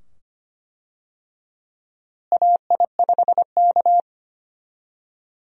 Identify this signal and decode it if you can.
Morse code, heard as a keyed tone, about 25 wpm, 700 Hz. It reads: AI5K
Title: USA Call Signs - 4 Characters - Repeated 2x - 25wpm